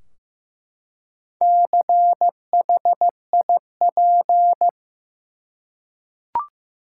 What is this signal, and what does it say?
Morse code, heard as a keyed tone, about 15 wpm, 700 Hz. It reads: CHIP
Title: Four Letter Words - 15wpm